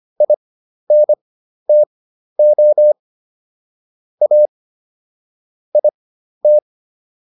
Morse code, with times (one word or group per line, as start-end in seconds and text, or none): 0.2-2.9 INTO
4.2-4.5 A
5.8-6.6 IT